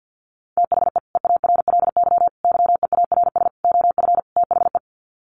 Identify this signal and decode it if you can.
Morse code, heard as a keyed tone, about 50 wpm, 700 Hz. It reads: THE EARLY YEARS OF THE
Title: Sets of 5 Words Encore 50wpm